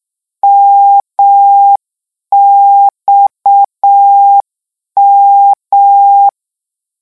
Morse code, heard as a keyed tone, about 6 wpm, 800 Hz. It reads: MXM
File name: morse mp3 / lecon 1-vitesse 6